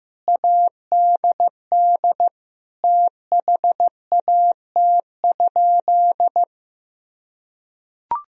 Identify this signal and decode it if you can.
Morse code, heard as a keyed tone, about 15 wpm, 700 Hz. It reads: ADD THAT?
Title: Sentences from Top 1000 Words - Review Part 1 - 15wpm